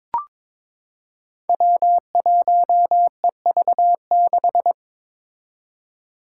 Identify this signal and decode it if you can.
Morse code, heard as a keyed tone, about 22 wpm, 700 Hz. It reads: W1EV6
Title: Five Challenging Letter-Number Combinations 22wpm